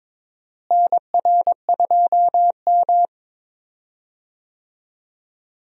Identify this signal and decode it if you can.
Morse code, heard as a keyed tone, about 22 wpm, 700 Hz. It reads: NR2M
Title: USA Call Signs - 4 Characters - Repeated 3x - 22wpm